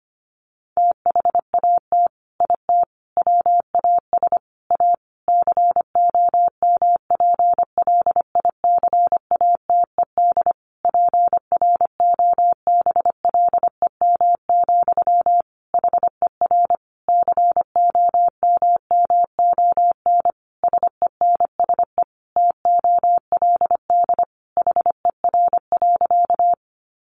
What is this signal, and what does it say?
Morse code, heard as a keyed tone, about 25 wpm, 700 Hz. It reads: THAT IT WAS A COMPLICATED PROBLEM, HER COMMON SENSE TOLD HER.
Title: Sentences from Top 1200 Words 25wpm